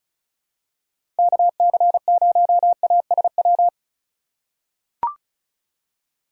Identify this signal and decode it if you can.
Morse code, heard as a keyed tone, about 34 wpm, 700 Hz. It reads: KC0ASW
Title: USA Call Signs - 6 Characters - 4x Character Spacing - 35wpm